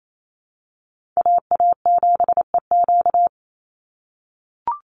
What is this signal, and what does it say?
Morse code, heard as a keyed tone, about 28 wpm, 700 Hz. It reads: AA7EQ
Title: NA SOTA Chaser Callsigns 28wpm